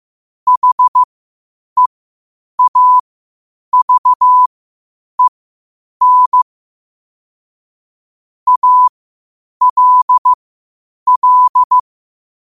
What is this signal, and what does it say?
Morse code, heard as a keyed tone, about 15 wpm, spaced as for 8 wpm, 1000 Hz. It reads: HEAVEN ALL
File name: qotdfiles / 8wpm